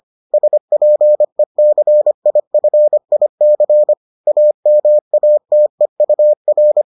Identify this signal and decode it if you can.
Morse code, heard as a keyed tone, about 25 wpm, 600 Hz. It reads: SPECIFIC AMATEUR